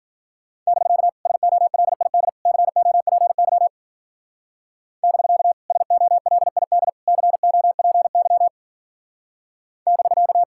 Morse code, heard as a keyed tone, about 46 wpm, 700 Hz. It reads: <BK> SOLID COPY <BK> SOLID COPY <BK>
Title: Speed-Racing Common QSO Phrases - 40wpm